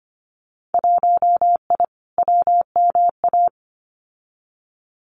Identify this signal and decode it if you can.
Morse code, heard as a keyed tone, about 25 wpm, 700 Hz. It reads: 1I WMA
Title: Winter Field Day Exchanges Encore 25wpm